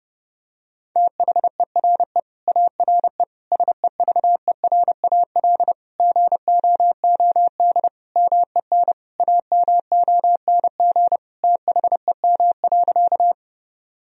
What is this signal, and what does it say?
Morse code, heard as a keyed tone, about 32 wpm, 700 Hz. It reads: THERE ARE SEVERAL GOOD MEN AMONG THEM.
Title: Sentences from Top 1000 Words - Review Part 2 - 30wpm